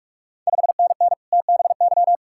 Morse code, heard as a keyed tone, about 46 wpm, 700 Hz. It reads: 5NN TBY
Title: Ontario QSO Party Exchange 45wpm